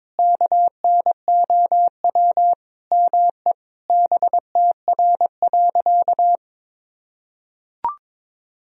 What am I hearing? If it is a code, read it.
Morse code, heard as a keyed tone, about 22 wpm, 700 Hz. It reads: KNOW ME BTR.
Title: Sentences from Top 400 Words and Abbreviations 22wpm